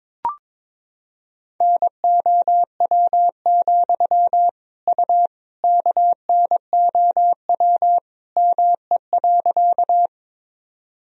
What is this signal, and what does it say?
Morse code, heard as a keyed tone, about 22 wpm, 700 Hz. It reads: NOW, U KNOW ME.
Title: Sentences from Top 200 Words and Abbreviations 22wpm